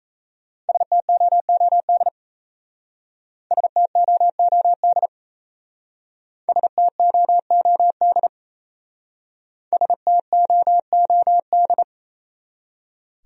Morse code, heard as a keyed tone, about 35 wpm, 700 Hz. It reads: STOOD STOOD STOOD STOOD
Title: Speed-Racing Top 500 Words 28wpm